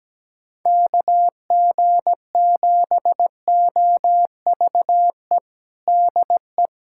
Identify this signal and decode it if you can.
Morse code, heard as a keyed tone, about 17 wpm, 700 Hz. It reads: KG7OVE DE
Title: USA Callsign DE Callsign 17wpm